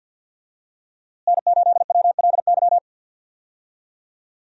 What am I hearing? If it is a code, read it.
Morse code, heard as a keyed tone, about 48 wpm, 700 Hz. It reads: N8WLY
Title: USA Call Signs - 5 Characters - Repeated 3x - 50wpm